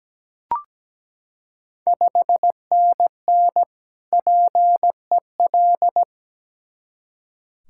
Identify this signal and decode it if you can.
Morse code, heard as a keyed tone, about 17 wpm, 690 Hz. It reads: 5NN PEL
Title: Ontario QSO Party Exchange 17wpm